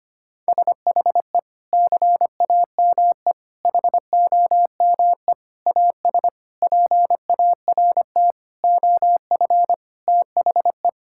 Morse code, heard as a keyed tone, about 25 wpm, 700 Hz. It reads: SHE CAME HOME AS PART OF THE
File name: practice / 25wpm / Sentences with 4-Word Phrases - 25wpm